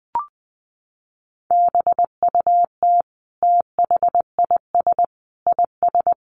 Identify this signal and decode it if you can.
Morse code, heard as a keyed tone, about 20 wpm, 700 Hz. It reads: BUT THIS IS
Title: Sentences from Top 1500 Words 20wpm